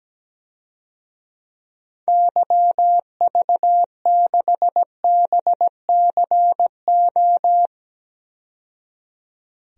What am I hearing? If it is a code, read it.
Morse code, heard as a keyed tone, about 17 wpm, 700 Hz. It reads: YV6BCO
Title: DXCC-100 Call Signs - 6 Characters - 17wpm